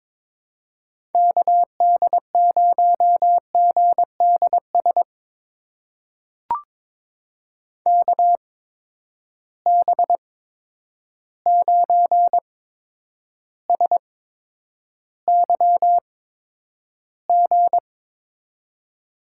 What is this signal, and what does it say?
Morse code, heard as a keyed tone, about 22 wpm, spaced as for 9 wpm, 700 Hz. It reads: KD0GDS KB9SYG
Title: USA Call Signs - 6 Characters - 8x Character Spacing - 22wpm